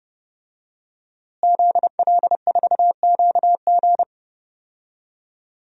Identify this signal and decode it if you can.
Morse code, heard as a keyed tone, about 30 wpm, 700 Hz. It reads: ZL4QG
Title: DXCC-100 Call Signs - 5 Characters - Repeated 2x - 30wpm